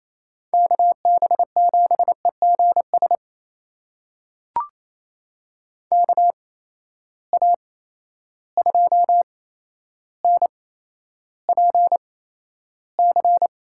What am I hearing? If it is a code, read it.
Morse code, heard as a keyed tone, about 28 wpm, spaced as for 12 wpm, 700 Hz. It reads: KB7EGS KA2NPC
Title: USA Call Signs - 6 Characters - 8x Character Spacing - 28wpm